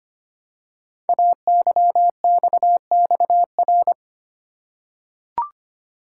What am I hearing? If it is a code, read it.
Morse code, heard as a keyed tone, about 25 wpm, 700 Hz. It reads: AYXXR
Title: Code Groups - 5 Letters - 25wpm